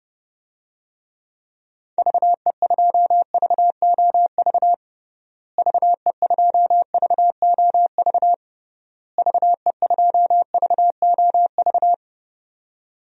Morse code, heard as a keyed tone, about 32 wpm, 700 Hz. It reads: VE2VOV VE2VOV VE2VOV
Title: DXCC-100 Call Signs - 6 Characters - Repeated 3x - 30wpm